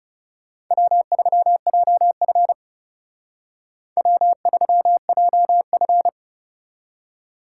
Morse code, heard as a keyed tone, about 36 wpm, 700 Hz. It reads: W3JF W3JF
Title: Speed Racing - USA Call Signs 30wpm